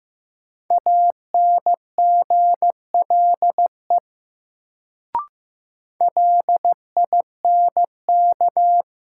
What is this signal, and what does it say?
Morse code, heard as a keyed tone, about 15 wpm, 700 Hz. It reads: ANGLE LINK
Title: Sentences from Top 1500 Words - New Word Drill Alt - 15wpm